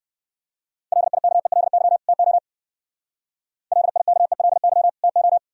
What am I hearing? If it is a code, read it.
Morse code, heard as a keyed tone, about 55 wpm, 700 Hz. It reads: LIKELY TO LIKELY TO
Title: Speed-Racing - Sets of 2 Words 45wpm